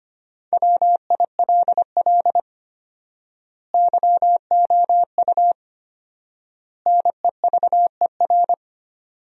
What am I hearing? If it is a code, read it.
Morse code, heard as a keyed tone, about 25 wpm, 700 Hz. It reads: WILL YOU NEVER
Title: 4x Word-Spacing - Sentences from Top 300 Words - 25wpm